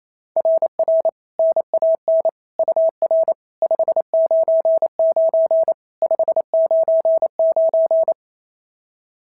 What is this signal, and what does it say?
Morse code, heard as a keyed tone, about 28 wpm, 660 Hz. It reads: RR NAN UR 599 599
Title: SOTA QSO Practice 28wpm